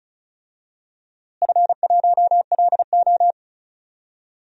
Morse code, heard as a keyed tone, about 36 wpm, 700 Hz. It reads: F1LO
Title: DXCC-100 Call Signs - 4 Characters - Repeated 2x - 35wpm